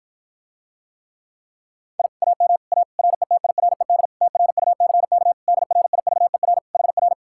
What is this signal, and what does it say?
Morse code, heard as a keyed tone, about 52 wpm, 700 Hz. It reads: I AM A RETIRED TRUCK DRIVER HR